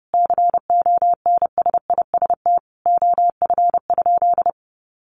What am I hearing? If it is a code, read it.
Morse code, heard as a keyed tone, about 30 wpm, 700 Hz. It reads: CONSIST OF?
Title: Sentences from Top 1400 Words 30wpm